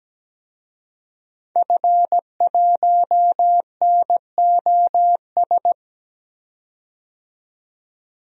Morse code, heard as a keyed tone, about 17 wpm, 700 Hz. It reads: F1NOS
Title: DXCC-100 Call Signs - 5 Characters - 17wpm